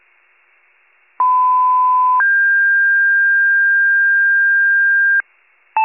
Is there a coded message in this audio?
Yes: two-tone page from 1.2 to 5.2 s, heard as a one-second tone, then a three-second tone, about 1010 then 1600 Hz.